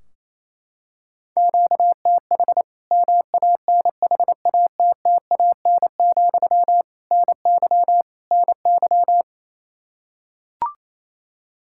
Morse code, heard as a keyed tone, about 28 wpm, 700 Hz. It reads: QTH MANHATTAN, NY NY
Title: Practice QSO - Exchange 1 - Beginning - 28wpm